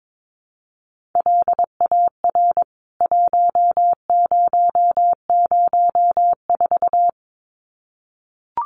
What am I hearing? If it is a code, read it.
Morse code, heard as a keyed tone, about 22 wpm, 700 Hz. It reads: LAR 1004
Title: CWT Exchange 22wpm